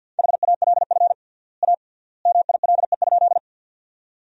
Morse code, heard as a keyed tone, about 53 wpm, 700 Hz. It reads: HALF A MILE?
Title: 3x Word-Spacing - Sentences from Top 400 Words - 50wpm